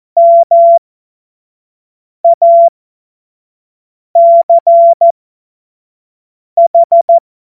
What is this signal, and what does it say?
Morse code, heard as a keyed tone, about 14 wpm, spaced as for 5 wpm, 680 Hz. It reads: MACH
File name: Morsekurs / Witze / Witz 12. April 2022 25 F70